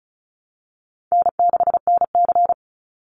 Morse code, heard as a keyed tone, about 34 wpm, 700 Hz. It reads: N6NC
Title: Easy Call Signs Repeated 3x - 35wpm